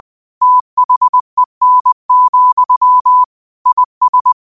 Morse code, heard as a keyed tone, about 20 wpm, 1000 Hz. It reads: THEN, IS
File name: qotdfiles / 20wpm